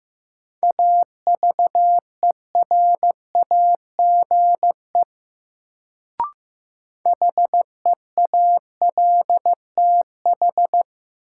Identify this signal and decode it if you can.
Morse code, heard as a keyed tone, about 15 wpm, 700 Hz. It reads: AVERAGE HEALTH
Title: Sets of 5 Words - New Word Drill - 15wpm